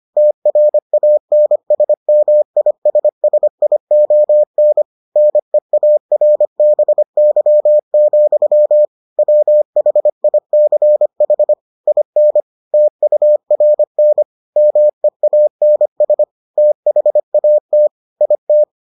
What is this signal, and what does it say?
Morse code, heard as a keyed tone, about 25 wpm, 600 Hz. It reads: TRANSMISSION NEARBY, WHICH IN TURN MEANS THAT IT